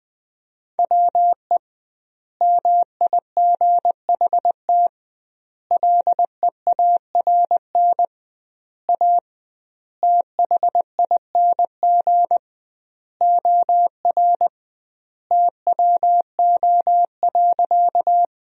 Morse code, heard as a keyed tone, about 21 wpm, 700 Hz. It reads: WE MIGHT LEARN A THING OR TWO.
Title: 2x Word-Spacing - Sentences from Top 300 Words - 20wpm